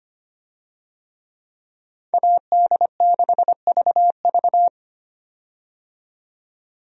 Morse code, heard as a keyed tone, about 25 wpm, 700 Hz. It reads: AD6VV
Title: USA Call Signs - 5 Characters - 25wpm